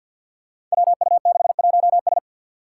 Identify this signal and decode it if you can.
Morse code, heard as a keyed tone, about 51 wpm, 700 Hz. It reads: WU61S
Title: Five Challenging Letter-Number Combinations 50wpm